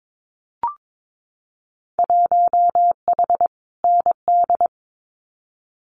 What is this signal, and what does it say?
Morse code, heard as a keyed tone, about 22 wpm, 700 Hz. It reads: 1H ND
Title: Winter Field Day Exchanges 22wpm